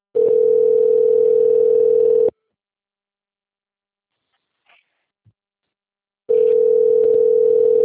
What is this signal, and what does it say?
Ringback tone (North America), 440 + 480 Hz.